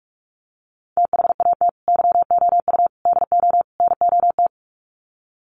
Morse code, heard as a keyed tone, about 44 wpm, 700 Hz. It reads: THAT YOU DO NOT
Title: Sets of 4 Words 45wpm